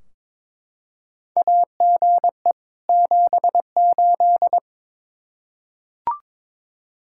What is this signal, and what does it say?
Morse code, heard as a keyed tone, about 22 wpm, 700 Hz. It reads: AGE 78
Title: K1USN SST QSO Practice 22wpm